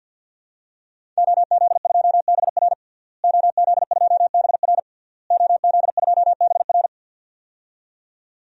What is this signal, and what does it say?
Morse code, heard as a keyed tone, about 52 wpm, 700 Hz. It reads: OZ2BR OZ2BR OZ2BR
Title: DXCC-100 Call Signs - 5 Characters - Repeated 3x - 50wpm